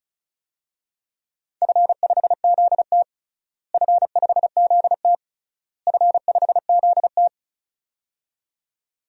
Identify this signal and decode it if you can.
Morse code, heard as a keyed tone, about 36 wpm, 700 Hz. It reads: F5ZT F5ZT F5ZT
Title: DXCC-100 Call Signs - 4 Characters - Repeated 3x - 35wpm